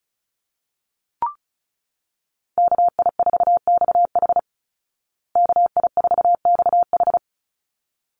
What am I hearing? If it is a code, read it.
Morse code, heard as a keyed tone, about 34 wpm, 700 Hz. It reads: KI4XH KI4XH
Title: Call Signs Repeated 3x - 35wpm